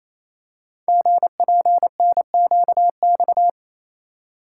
Morse code, heard as a keyed tone, about 29 wpm, 700 Hz. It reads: GPNQX
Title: Code Groups - 5 Letter:Numbers - 28wpm